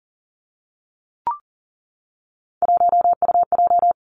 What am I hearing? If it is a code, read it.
Morse code, heard as a keyed tone, about 41 wpm, 700 Hz. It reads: 1UJ
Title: Three Challenging Letter-Number Combinations 40wpm